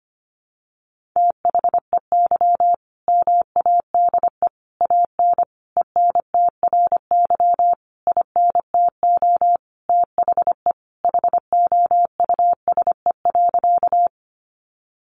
Morse code, heard as a keyed tone, about 25 wpm, 700 Hz. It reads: THEY MADE AN ENTRY INTO THE HOUSE.
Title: Sentences from Top 1500 Words 25wpm